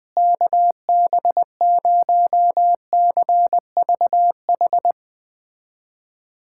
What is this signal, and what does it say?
Morse code, heard as a keyed tone, about 20 wpm, 700 Hz. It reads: KB0CVH
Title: USA Callsign DE Callsign 20wpm